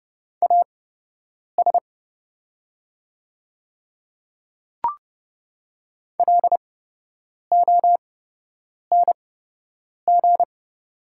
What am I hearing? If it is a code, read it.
Morse code, heard as a keyed tone, about 31 wpm, spaced as for 8 wpm, 700 Hz. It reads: AS LONG
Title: 8x Character Spacing - Top 100 Words - 30wpm